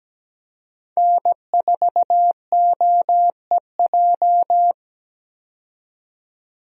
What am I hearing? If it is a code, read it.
Morse code, heard as a keyed tone, about 17 wpm, 700 Hz. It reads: N4OEJ
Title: USA Call Signs - 5 Characters - 17wpm